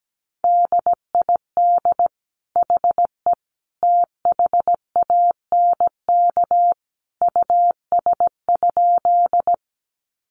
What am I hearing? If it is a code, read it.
Morse code, heard as a keyed tone, about 17 wpm, 700 Hz. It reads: DID HE THANK US?
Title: Sentences from Top 600 Words 17wpm